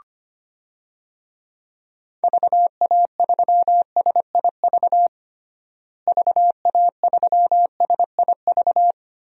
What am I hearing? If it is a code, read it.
Morse code, heard as a keyed tone, about 25 wpm, 700 Hz. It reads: VA3SIV VA3SIV
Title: DXCC-100 Call Signs - 6 Characters - Repeated 2x - 25wpm